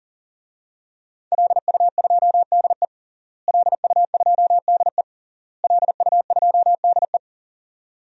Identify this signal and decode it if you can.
Morse code, heard as a keyed tone, about 40 wpm, 700 Hz. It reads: LU2DE LU2DE LU2DE
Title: DXCC-100 Call Signs - 5 Characters - Repeated 3x - 40wpm